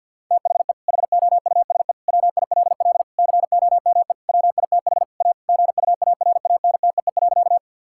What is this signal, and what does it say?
Morse code, heard as a keyed tone, about 50 wpm, 700 Hz. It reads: THE HOUSE WILL COME WITH A GUARANTEE.
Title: Sentences from Top 1600 Words 50wpm